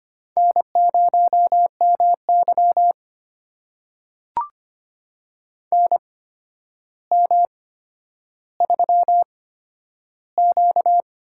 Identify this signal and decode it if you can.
Morse code, heard as a keyed tone, about 25 wpm, spaced as for 11 wpm, 700 Hz. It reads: N0MY NM3Q
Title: USA Call Signs - 4 Characters - 8x Character Spacing - 25wpm